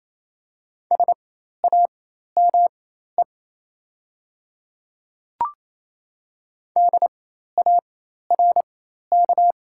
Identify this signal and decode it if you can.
Morse code, heard as a keyed tone, about 28 wpm, spaced as for 13 wpm, 700 Hz. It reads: SAME DARK
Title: 4x Character Spacing - Top 500 Words - 28wpm